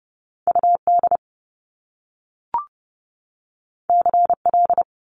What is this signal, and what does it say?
Morse code, heard as a keyed tone, about 30 wpm, 700 Hz. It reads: UD CL
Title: Top Two Letter Combinations in Words 30wpm